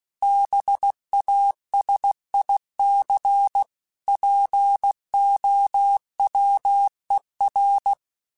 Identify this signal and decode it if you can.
Morse code, heard as a keyed tone, about 16 wpm, 800 Hz. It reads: BASIC POWER
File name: Technical Subjects / Technical Subjects-mp3s / Sw Pwr Sup @15